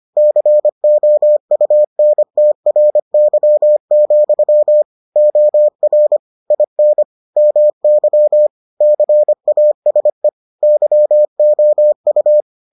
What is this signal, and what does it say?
Morse code, heard as a keyed tone, about 25 wpm, 600 Hz. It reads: COUNTRY, OR IN MY CASE YOU